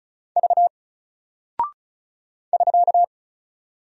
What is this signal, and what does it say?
Morse code, heard as a keyed tone, about 36 wpm, 700 Hz. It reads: V <SK>
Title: Single Character (Letter:Number:Punctuation:Prosign) Alt 35wpm